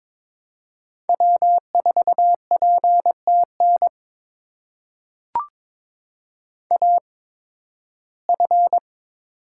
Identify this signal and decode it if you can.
Morse code, heard as a keyed tone, about 22 wpm, spaced as for 14 wpm, 700 Hz. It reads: W4PTN AF
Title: USA Call Signs - 5 Characters - 8x Character Spacing - 22wpm